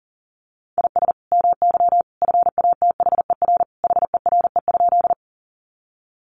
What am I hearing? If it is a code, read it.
Morse code, heard as a keyed tone, about 40 wpm, 700 Hz. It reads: IS MY FATHER HERE?
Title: Sentences from Top 300 Words 40wpm